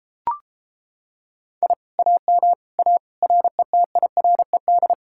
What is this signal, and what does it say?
Morse code, heard as a keyed tone, about 33 wpm, 700 Hz. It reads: I AM A RETIRED